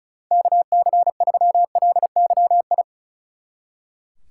Morse code, heard as a keyed tone, about 33 wpm, 700 Hz. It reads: KC3LYI
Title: Speed Racing - USA 2x3 Call Signs 35wpm